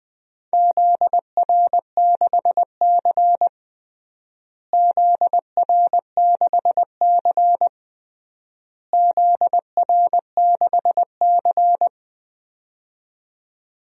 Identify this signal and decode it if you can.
Morse code, heard as a keyed tone, about 20 wpm, 700 Hz. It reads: ZR6C ZR6C ZR6C
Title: DXCC-100 Call Signs - 4 Characters - Repeated 3x - 20wpm